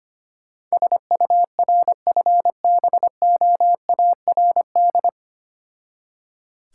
Morse code, heard as a keyed tone, about 25 wpm, 700 Hz. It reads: SURFBOARD
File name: practice / 25wpm / Difficult Words with Hints - 25wpm